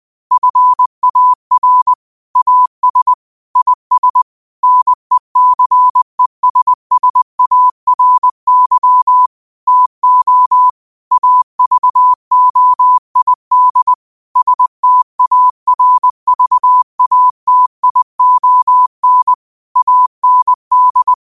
Morse code, heard as a keyed tone, about 20 wpm, 1000 Hz. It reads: FAR AS IS NECESSARY TO AVOID STARVATION AND
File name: qotdfiles / 20wpm